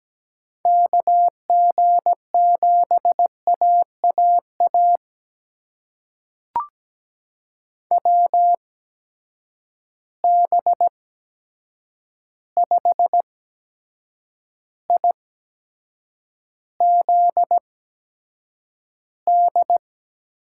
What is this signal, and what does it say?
Morse code, heard as a keyed tone, about 17 wpm, spaced as for 7 wpm, 700 Hz. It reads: KG7AAA WB5IZD
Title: USA Call Signs - 6 Characters - 8x Character Spacing - 17wpm